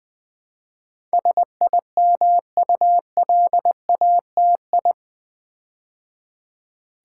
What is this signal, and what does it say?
Morse code, heard as a keyed tone, about 20 wpm, 700 Hz. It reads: SIMULATI
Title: Build A Word - 4 Letters At A Time - 10-Letter Words - 20wpm